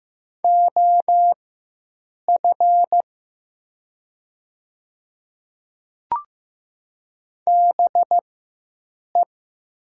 Morse code, heard as a keyed tone, about 15 wpm, spaced as for 7 wpm, 700 Hz. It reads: OF BE
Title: 4x Character Spacing - Top 100 Words - 15wpm